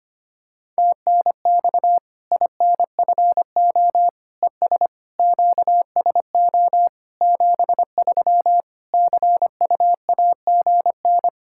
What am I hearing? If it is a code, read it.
Morse code, heard as a keyed tone, about 25 wpm, 700 Hz. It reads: TNX INFO ES QSO 73 CUAGN
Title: Practice QSO - Exchange 4 - Ending - 25wpm